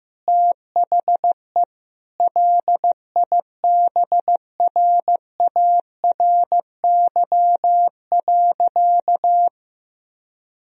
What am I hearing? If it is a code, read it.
Morse code, heard as a keyed tone, about 16 wpm, 700 Hz. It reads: THE LIBRARY.
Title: Sentences from Top 1000 Words - Review Part 4 - 15wpm